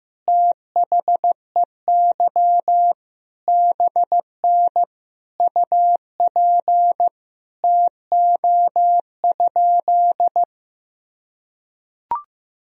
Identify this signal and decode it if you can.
Morse code, heard as a keyed tone, about 15 wpm, 700 Hz. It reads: THEY BN UP TO?
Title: Sentences from Top 100 Words and Abbreviations 15wpm